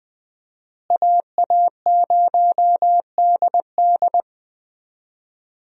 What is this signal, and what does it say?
Morse code, heard as a keyed tone, about 20 wpm, 700 Hz. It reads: AA0DD
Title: USA Call Signs - 5 Characters - Repeated 3x - 20wpm